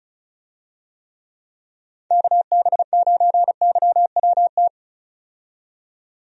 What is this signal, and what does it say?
Morse code, heard as a keyed tone, about 34 wpm, 700 Hz. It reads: KB9YWT
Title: USA Call Signs - 6 Characters - 35wpm